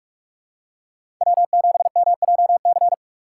Morse code, heard as a keyed tone, about 46 wpm, 700 Hz. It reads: W7MJC
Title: USA Call Signs - 5 Characters - 2x Character Spacing - 45wpm